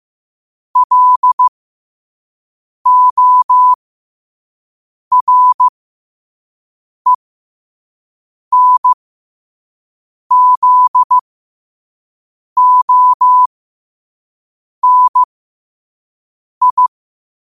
Morse code, heard as a keyed tone, about 15 wpm, spaced as for 5 wpm, 1000 Hz. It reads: LORENZONI
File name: qotdfiles / 5wpm